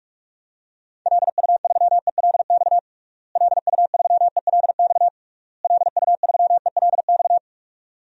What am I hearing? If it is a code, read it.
Morse code, heard as a keyed tone, about 46 wpm, 700 Hz. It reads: LU3ELX LU3ELX LU3ELX
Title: DXCC-100 Call Signs - 6 Characters - Repeated 3x - 45wpm